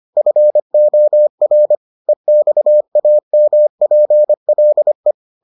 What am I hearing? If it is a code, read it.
Morse code, heard as a keyed tone, about 25 wpm, 600 Hz. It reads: FOR EXAMPLE